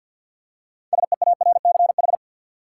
Morse code, heard as a keyed tone, about 53 wpm, 700 Hz. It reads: SEARCH
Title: Six Letter Words - 50wpm